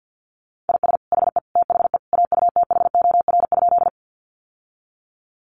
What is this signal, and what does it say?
Morse code, heard as a keyed tone, about 50 wpm, 700 Hz. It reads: IS HE THE AUTHOR?
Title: Sentences from Top 1200 Words 50wpm